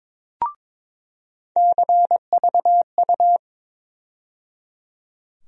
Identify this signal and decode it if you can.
Morse code, heard as a keyed tone, about 22 wpm, 700 Hz. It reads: CVU
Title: USA Call Sign Postfixes - 3 Letters 22wpm